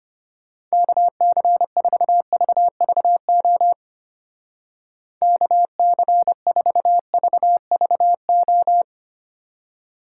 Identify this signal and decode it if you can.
Morse code, heard as a keyed tone, about 29 wpm, 700 Hz. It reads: KC4VVO KC4VVO
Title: Speed Racing - USA 2x3 Call Signs 25wpm